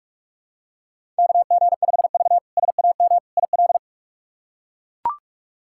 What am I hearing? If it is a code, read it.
Morse code, heard as a keyed tone, about 43 wpm, 700 Hz. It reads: KG5V SAM IL
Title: K1USN SST QSO Practice - 45wpm